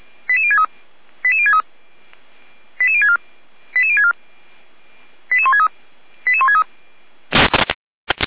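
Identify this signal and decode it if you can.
Five-tone selective call (ZVEI), heard as quick runs of five tones, tones of about 70 ms. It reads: 88063 88063 88064 88064 88163 88163
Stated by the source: ZVEI five-tone selective call received on an SDR.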